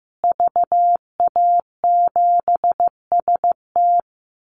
Morse code, heard as a keyed tone, about 15 wpm, 700 Hz. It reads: VA7ST
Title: Call Signs Repeated 3x - 15wpm